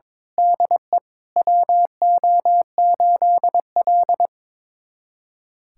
Morse code, heard as a keyed tone, about 23 wpm, 700 Hz. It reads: DE WO8L
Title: USA Callsign DE Callsign - 4 Characters - 22wpm